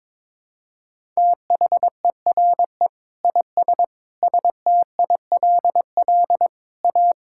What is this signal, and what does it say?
Morse code, heard as a keyed tone, about 22 wpm, 700 Hz. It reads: THERE IS STILL A
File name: Sentences with 9-Word Phrases - 22wpm